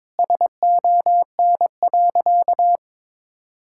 Morse code, heard as a keyed tone, about 22 wpm, 700 Hz. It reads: SON.
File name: practice / 22wpm / Sentences with 8-Word Phrases - 22wpm